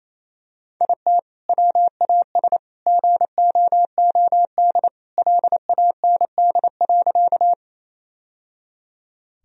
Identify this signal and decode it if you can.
Morse code, heard as a keyed tone, about 28 wpm, 700 Hz. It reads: IT WAS GOOD LAND.
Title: Sentences from Top 1000 Words - Review Part 1 - 28wpm